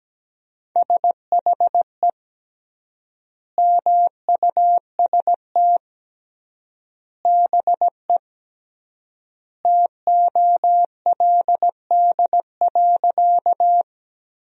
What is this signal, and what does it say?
Morse code, heard as a keyed tone, about 17 wpm, 700 Hz. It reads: SHE MUST BE TOLD.
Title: 3x Word-Spacing - Sentences from Top 400 Words - 17wpm